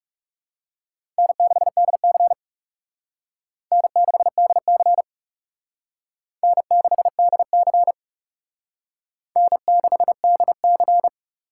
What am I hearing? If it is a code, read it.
Morse code, heard as a keyed tone, about 37 wpm, 700 Hz. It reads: N6DC N6DC N6DC N6DC
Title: Speed Racing - USA 1x2 Call Signs 30wpm